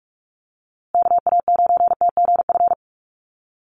Morse code, heard as a keyed tone, about 44 wpm, 690 Hz. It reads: KA9TGF
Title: USA Call Signs Repeated 2x 45wpm